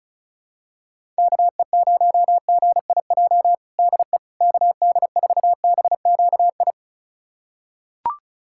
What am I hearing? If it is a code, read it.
Morse code, heard as a keyed tone, about 36 wpm, 700 Hz. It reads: KE0GIJ DE KD4BQI
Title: USA Callsign DE Callsign - 6 Characters - 35wpm